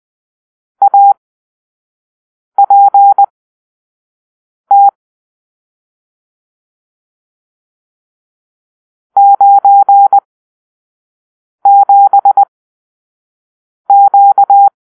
Morse code, heard as a keyed tone, about 20 wpm, spaced as for 5 wpm, 800 Hz. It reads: APT 97Q
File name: arrl / Grp-7 Drill-3